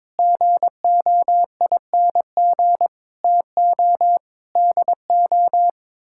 Morse code, heard as a keyed tone, about 22 wpm, 700 Hz. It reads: GOING TO DO
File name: practice / 22wpm / Sets of 6 Words - 22wpm